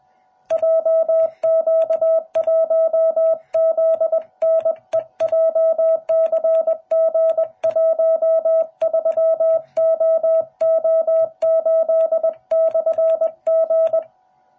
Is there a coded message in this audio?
Yes: Morse code from 0.5 to 14.0 s, heard as a keyed tone, about 21 wpm, 650 Hz. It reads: JQ1ZNE JCG13OO8/G